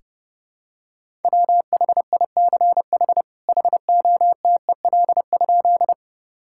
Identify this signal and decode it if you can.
Morse code, heard as a keyed tone, about 30 wpm, 700 Hz. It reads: WHICH HOTEL?
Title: Sentences from Top 1600 Words 30wpm